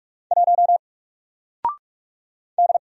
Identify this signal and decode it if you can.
Morse code, heard as a keyed tone, about 44 wpm, 700 Hz. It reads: 1 D
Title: Single Letter-Number Alt 45wpm